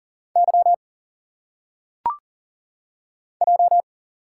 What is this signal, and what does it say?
Morse code, heard as a keyed tone, about 40 wpm, 700 Hz. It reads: Y J
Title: Single Letters - Phonetic Pronunciation 40wpm